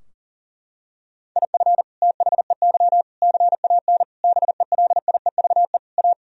Morse code, heard as a keyed tone, about 40 wpm, 700 Hz. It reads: IF THEY CAN BELIEVE A
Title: Sentences from Top 1000 Words - Review Part 2 - 40wpm